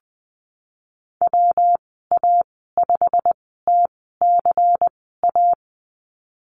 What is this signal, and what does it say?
Morse code, heard as a keyed tone, about 20 wpm, spaced as for 15 wpm, 700 Hz. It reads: WA5TCA
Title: 193 Lesson 31g - Letter v - Call Signs 20wpm